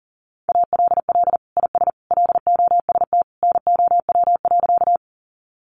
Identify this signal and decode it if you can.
Morse code, heard as a keyed tone, about 37 wpm, 700 Hz. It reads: ALL IS LOST NOW.
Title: Sentences from Top 1400 Words 40wpm